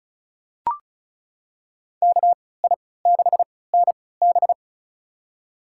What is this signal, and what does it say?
Morse code, heard as a keyed tone, about 36 wpm, spaced as for 20 wpm, 700 Hz. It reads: KI6NB
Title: USA Call Signs - 3x Character Spacing - 35wpm